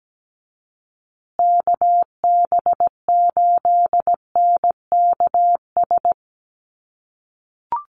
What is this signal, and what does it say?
Morse code, heard as a keyed tone, about 17 wpm, 700 Hz. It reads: KB8NKS
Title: USA Call Signs Repeated 3x 17wpm